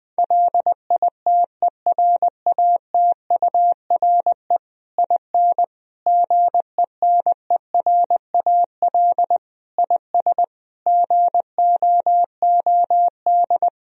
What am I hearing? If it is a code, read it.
Morse code, heard as a keyed tone, about 20 wpm, 700 Hz. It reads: LITERATURE IN GENERAL IS GOOD
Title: Sentences from Top 1000 Words - Review Part 3 - 20wpm